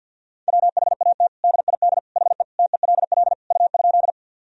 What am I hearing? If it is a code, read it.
Morse code, heard as a keyed tone, about 51 wpm, 700 Hz. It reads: WHAT DID HE TELL U?
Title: Sentences from Top 200 Words and Abbreviations 50wpm